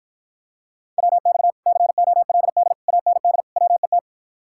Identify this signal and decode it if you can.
Morse code, heard as a keyed tone, about 52 wpm, 700 Hz. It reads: WX COLD AND WET